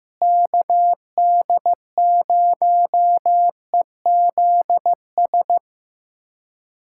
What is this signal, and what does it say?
Morse code, heard as a keyed tone, about 15 wpm, 700 Hz. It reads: KD0EZS
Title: USA Callsign DE Callsign 15wpm